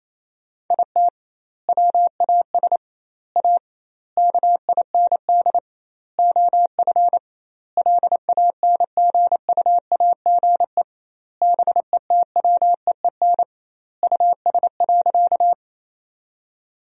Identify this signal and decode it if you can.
Morse code, heard as a keyed tone, about 28 wpm, 700 Hz. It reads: IT WAS A KIND OF LANGUAGE BETWEEN US.
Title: 2x Word-Spacing - Sentences from Top 500 Words - 28wpm